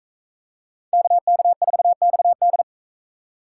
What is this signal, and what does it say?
Morse code, heard as a keyed tone, about 40 wpm, 700 Hz. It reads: KK4XD